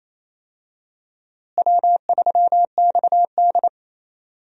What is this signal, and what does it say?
Morse code, heard as a keyed tone, about 27 wpm, 700 Hz. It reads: W3XD
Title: USA Call Signs - 4 Characters - Repeated 3x - 28wpm